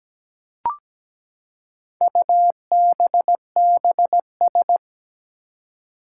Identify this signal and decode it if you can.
Morse code, heard as a keyed tone, about 17 wpm, 700 Hz. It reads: UBBS
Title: Four Challenging Letter-Number Combinations 17wpm